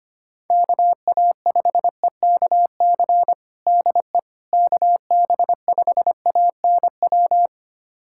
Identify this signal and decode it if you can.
Morse code, heard as a keyed tone, about 25 wpm, 700 Hz. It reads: KA5EKC DE KB5ANW
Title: USA Callsign DE Callsign - 6 Characters - 25wpm